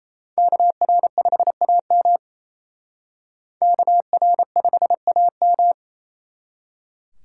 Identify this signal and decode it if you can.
Morse code, heard as a keyed tone, about 35 wpm, 700 Hz. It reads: KR5AM KR5AM
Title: Speed Racing - USA 2x2 Call Signs 28wpm